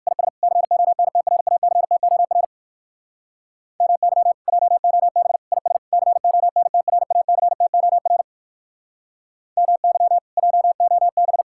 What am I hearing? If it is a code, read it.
Morse code, heard as a keyed tone, about 54 wpm, 700 Hz. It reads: IS CONTRACTOR MY JOB IS CONTRACTOR MY JOB
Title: Speed-Racing Common QSO Phrases - 45wpm